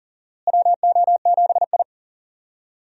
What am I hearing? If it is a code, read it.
Morse code, heard as a keyed tone, about 40 wpm, 700 Hz. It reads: WO7I
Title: USA Call Signs - 4 Characters - Repeated 3x - 40wpm